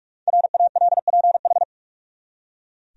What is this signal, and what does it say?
Morse code, heard as a keyed tone, about 44 wpm, 700 Hz. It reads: RALPH
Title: USA Common Names - Max 5 Letters - 45wpm